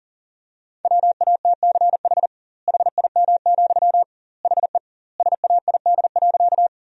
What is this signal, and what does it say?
Morse code, heard as a keyed tone, about 40 wpm, 700 Hz. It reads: WATCH HIM, HE SAID.
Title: 2x Word-Spacing - Sentences from Top 300 Words - 40wpm